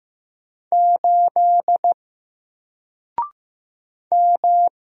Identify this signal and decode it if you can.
Morse code, heard as a keyed tone, about 15 wpm, 700 Hz. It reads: 8 M
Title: Single Letter-Number Alt 15wpm